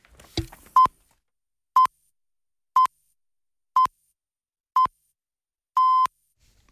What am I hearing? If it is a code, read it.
Radio time signal (time pips). It is heard from 0.8 to 6.1 s.